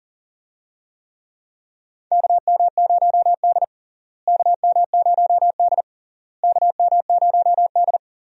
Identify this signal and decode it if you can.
Morse code, heard as a keyed tone, about 40 wpm, 700 Hz. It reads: KM0D KM0D KM0D
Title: USA Call Signs - 4 Characters - Repeated 3x - 40wpm